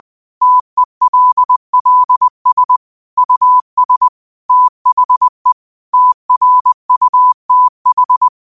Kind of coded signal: Morse code